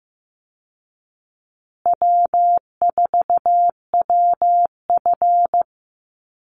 Morse code, heard as a keyed tone, about 15 wpm, 700 Hz. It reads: W4WF
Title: Easy Call Signs Repeated 2x - 15wpm